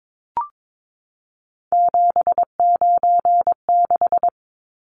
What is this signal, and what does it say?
Morse code, heard as a keyed tone, about 22 wpm, 700 Hz. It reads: 796
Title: Numbers - 3-Digit 22wpm